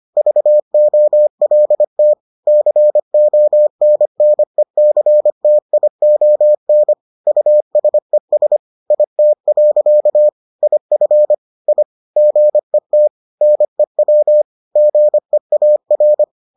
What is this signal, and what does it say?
Morse code, heard as a keyed tone, about 25 wpm, 600 Hz. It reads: VOLT CONNECTION USES IT. IF I GET NEW GEAR